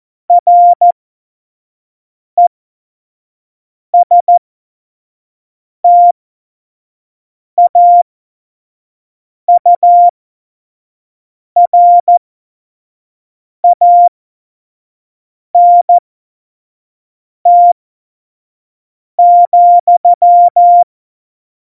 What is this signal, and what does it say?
Morse code, heard as a keyed tone, about 14 wpm, spaced as for 5 wpm, 700 Hz. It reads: RESTAURANT,